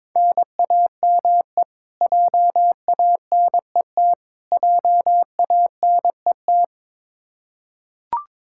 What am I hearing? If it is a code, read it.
Morse code, heard as a keyed tone, about 22 wpm, 700 Hz. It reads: NAME JANET JANET
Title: Practice QSO - Exchange 1 - Beginning - 22wpm